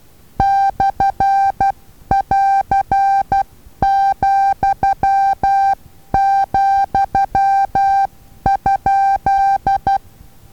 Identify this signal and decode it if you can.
Morse code, heard as a keyed tone, about 12 wpm, 790 Hz. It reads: /<AR>,,?
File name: Taviro HA6VA A